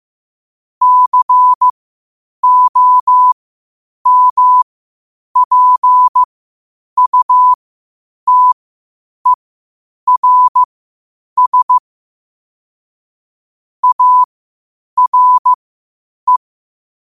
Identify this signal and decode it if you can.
Morse code, heard as a keyed tone, about 15 wpm, spaced as for 8 wpm, 1000 Hz. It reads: COMPUTERS ARE
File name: qotdfiles / 8wpm